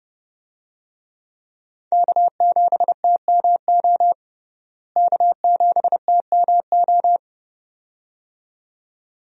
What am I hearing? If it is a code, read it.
Morse code, heard as a keyed tone, about 32 wpm, 700 Hz. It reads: K7TMO K7TMO
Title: USA Call Signs - 5 Characters - Repeated 2x - 30wpm